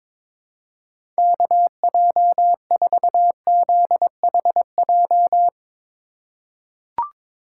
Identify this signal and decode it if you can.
Morse code, heard as a keyed tone, about 22 wpm, 700 Hz. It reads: KJ4ZHJ
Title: USA Call Signs - 6 Characters - 4x Character Spacing - 22wpm